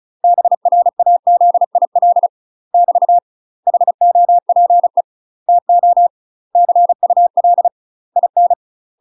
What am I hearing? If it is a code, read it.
Morse code, heard as a keyed tone, about 34 wpm, 700 Hz. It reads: BRAZIL <BT> HOPE TO CUL IN
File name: UK MASTER QSOs TRAINING / 35WPM / UK MASTER QSOs 0021